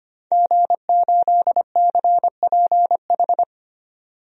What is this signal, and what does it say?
Morse code, heard as a keyed tone, about 25 wpm, 690 Hz. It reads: G8CPH
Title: DXCC-100 Call Signs - 5 Characters - 25wpm